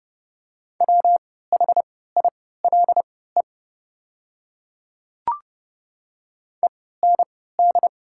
Morse code, heard as a keyed tone, about 30 wpm, spaced as for 17 wpm, 690 Hz. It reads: WHILE END
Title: 3x Character Spacing - Top 300 Words - 30wpm